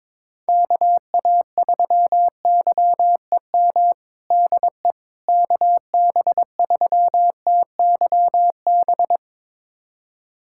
Morse code, heard as a keyed tone, about 21 wpm, 700 Hz. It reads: KA3YEM DE KB3TYB
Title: USA Callsign DE Callsign 22wpm